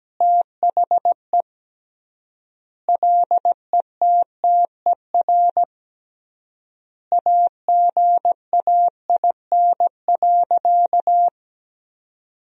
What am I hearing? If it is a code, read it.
Morse code, heard as a keyed tone, about 17 wpm, 700 Hz. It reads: THE LETTER AGAIN.
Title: 3x Word-Spacing - Sentences from Top 300 Words - 17wpm